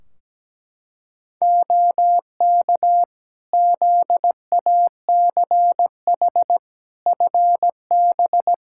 Morse code, heard as a keyed tone, about 17 wpm, 700 Hz. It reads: OK ZACH FB
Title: Practice QSO - Exchange 2 - Info - 17wpm